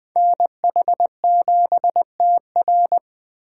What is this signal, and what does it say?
Morse code, heard as a keyed tone, about 24 wpm, 700 Hz. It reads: NH7TR
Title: Speed Racing - USA 2x2 Call Signs 20wpm